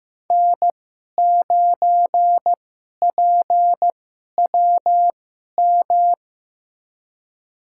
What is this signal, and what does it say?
Morse code, heard as a keyed tone, about 15 wpm, spaced as for 11 wpm, 700 Hz. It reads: N9PWM
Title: USA Call Signs - 5 Characters - 2x Character Spacing - 15wpm